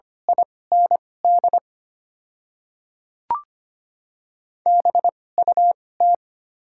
Morse code, heard as a keyed tone, about 25 wpm, spaced as for 18 wpm, 700 Hz. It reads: IND BUT
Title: 2x Character Spacing - Top 200 Words - 25wpm